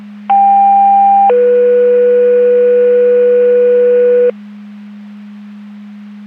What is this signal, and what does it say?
Two-tone page, heard as a one-second tone, then a three-second tone, about 790 then 490 Hz. It is heard from 0.3 to 4.3 s.
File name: Portage 154.385 Paging